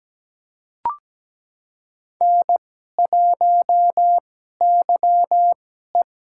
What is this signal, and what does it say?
Morse code, heard as a keyed tone, about 17 wpm, spaced as for 12 wpm, 700 Hz. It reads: N1YE
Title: USA Call Signs - 4 Characters - 2x Character Spacing - 17wpm